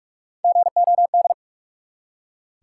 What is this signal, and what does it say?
Morse code, heard as a keyed tone, about 44 wpm, 700 Hz. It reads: GOD
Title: Three Letter Words - 45wpm